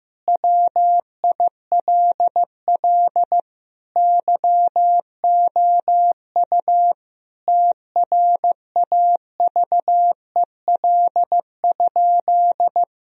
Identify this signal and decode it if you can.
Morse code, heard as a keyed tone, about 16 wpm, 700 Hz. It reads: WILL YOU TRAVEL?
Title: Sentences from Top 1000 Words - Review Part 2 - 15wpm